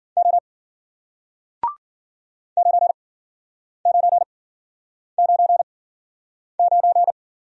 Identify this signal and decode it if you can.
Morse code, heard as a keyed tone, about 54 wpm, spaced as for 9 wpm, 700 Hz. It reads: O 9999